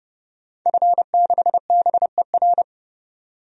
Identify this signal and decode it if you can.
Morse code, heard as a keyed tone, about 30 wpm, 700 Hz. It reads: F6BER
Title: DXCC-100 Call Signs - 5 Characters - 30wpm